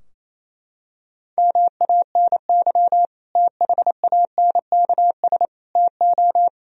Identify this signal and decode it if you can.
Morse code, heard as a keyed tone, about 28 wpm, 700 Hz. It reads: MANY THANKS TO
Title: Sentences from Top 1000 Words - Review Part 3 - 28wpm